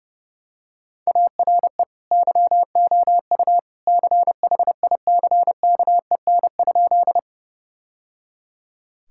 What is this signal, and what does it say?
Morse code, heard as a keyed tone, about 30 wpm, 700 Hz. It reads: ARE YOU CHICKEN?
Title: Sentences from Top 1000 Words - Review Part 5 - 30wpm